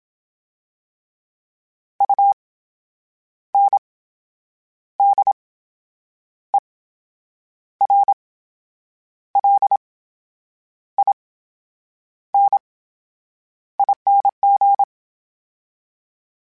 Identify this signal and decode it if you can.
Morse code, heard as a keyed tone, about 27 wpm, spaced as for 8 wpm, 800 Hz. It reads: UNDERLINING
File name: Special INGs / Special INGs-mp3s / VerySlowToFaster Ing @25